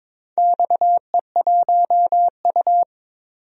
Morse code, heard as a keyed tone, about 22 wpm, 700 Hz. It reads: XE1U
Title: DXCC-100 Call Signs - 4 Characters - 22wpm